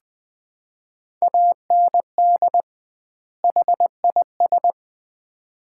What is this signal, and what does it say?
Morse code, heard as a keyed tone, about 20 wpm, 700 Hz. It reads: AND HIS
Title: 2x Word-Spacing - Sets of 2 Words - 20wpm